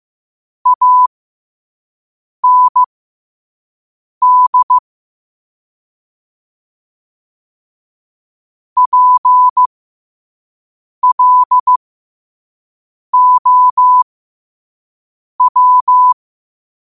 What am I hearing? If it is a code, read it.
Morse code, heard as a keyed tone, about 15 wpm, spaced as for 5 wpm, 1000 Hz. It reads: AND PLOW
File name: qotdfiles / 5wpm